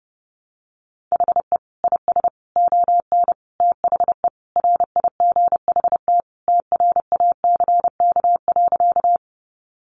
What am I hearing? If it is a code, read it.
Morse code, heard as a keyed tone, about 30 wpm, 700 Hz. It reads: HE IS ON THE RIGHT TRACK.
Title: Sentences from Top 1100 Words 30wpm